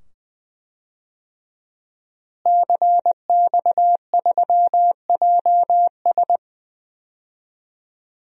Morse code, heard as a keyed tone, about 20 wpm, 700 Hz. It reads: CX3JS
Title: DXCC-100 Call Signs - 5 Characters - Repeated 2x - 20wpm